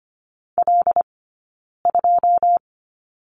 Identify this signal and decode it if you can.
Morse code, heard as a keyed tone, about 25 wpm, 700 Hz. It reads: L 2
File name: Lesson 15d - Number 5 - Two Characters 25wpm